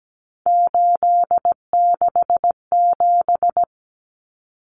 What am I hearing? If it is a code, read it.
Morse code, heard as a keyed tone, about 17 wpm, 700 Hz. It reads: 867
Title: Common Cut Numbers - 3-Digit 17wpm